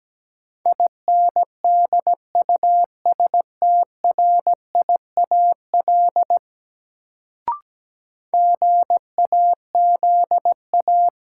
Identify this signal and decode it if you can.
Morse code, heard as a keyed tone, about 17 wpm, 700 Hz. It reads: INDUSTRIAL GAZA
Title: Sets of 5 Words - New Word Drill - 17wpm